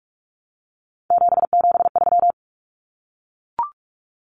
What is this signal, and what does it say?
Morse code, heard as a keyed tone, about 44 wpm, 700 Hz. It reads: 773
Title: Common Cut Numbers - 3-Digit 45wpm